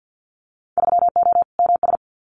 Morse code, heard as a keyed tone, about 50 wpm, 700 Hz. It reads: 3O MS
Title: Winter Field Day Exchanges Encore 50wpm